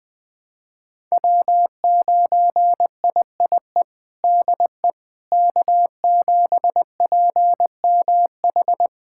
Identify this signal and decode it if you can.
Morse code, heard as a keyed tone, about 21 wpm, 700 Hz. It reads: W9IIE DE K7PMH
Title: USA Callsign DE Callsign - 5 Characters - 20wpm